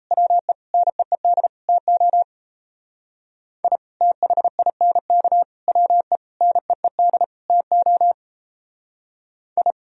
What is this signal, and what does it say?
Morse code, heard as a keyed tone, about 31 wpm, 700 Hz. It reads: WE NEED TO I THINK WE NEED TO I
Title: Speed-Racing - Sets of 5 Words 28wpm